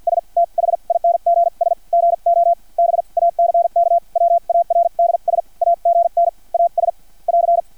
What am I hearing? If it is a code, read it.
Morse code, heard as a keyed tone, about 50 wpm, spaced as for 41 wpm, 680 Hz. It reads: S THINGS GO BACKWARDS AGN AS <AR>
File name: archived / CRSnet